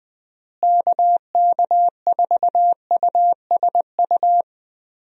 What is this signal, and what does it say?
Morse code, heard as a keyed tone, about 20 wpm, 700 Hz. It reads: KK4USU
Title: USA Call Signs - 6 Characters - Repeated 3x - 20wpm